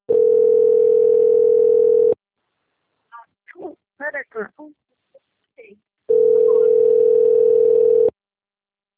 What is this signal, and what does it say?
Ringback tone (North America), 440 + 480 Hz, with speech.
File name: out- - - - - .12800